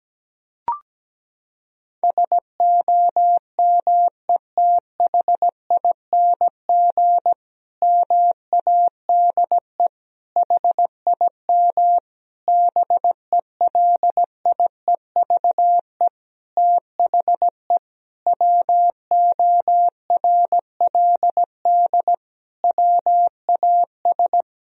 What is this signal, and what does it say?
Morse code, heard as a keyed tone, about 17 wpm, 700 Hz. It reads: SOMETHING MADE HIM BELIEVE THE WORLD WAS
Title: Sentences from Top 1000 Words - Review Part 1 - 17wpm